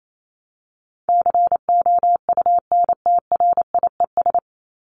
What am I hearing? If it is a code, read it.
Morse code, heard as a keyed tone, about 28 wpm, 700 Hz. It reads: COUNTRIES
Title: ES:S Suffix - Top 100 Words with Base Words 28wpm